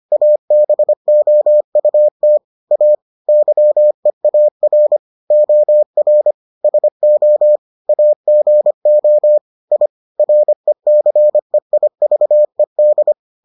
Morse code, heard as a keyed tone, about 25 wpm, 600 Hz. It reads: ABOUT A YEAR OR SO AGO I RECEIVED